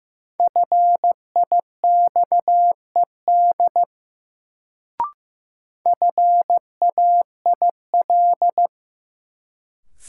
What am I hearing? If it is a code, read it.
Morse code, heard as a keyed tone, about 15 wpm, 700 Hz. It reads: FIXED FAIL
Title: Sentences from Top 1100 Words - New Word Drill Alt - 15wpm